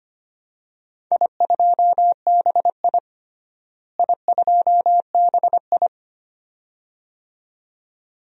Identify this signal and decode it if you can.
Morse code, heard as a keyed tone, about 25 wpm, 700 Hz. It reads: I2BI I2BI
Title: DXCC-100 Call Signs - 4 Characters - Repeated 2x - 25wpm